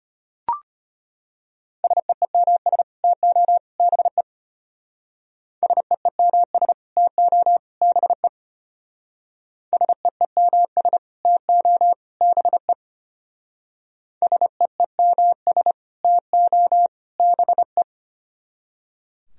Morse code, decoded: SEEMS TO BE SEEMS TO BE SEEMS TO BE SEEMS TO BE

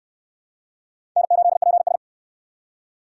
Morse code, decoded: N8PS

